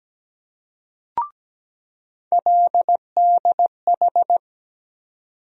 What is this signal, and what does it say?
Morse code, heard as a keyed tone, about 17 wpm, 700 Hz. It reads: LDH
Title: USA Call Sign Postfixes - 3 Letters 17wpm